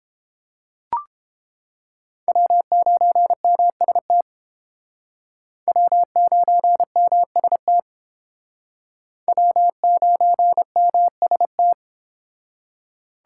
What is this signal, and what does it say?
Morse code, heard as a keyed tone, about 32 wpm, 700 Hz. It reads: W9MST W9MST W9MST